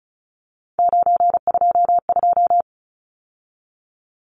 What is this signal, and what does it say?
Morse code, heard as a keyed tone, about 34 wpm, 700 Hz. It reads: 922
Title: Numbers - 3-Digit 35wpm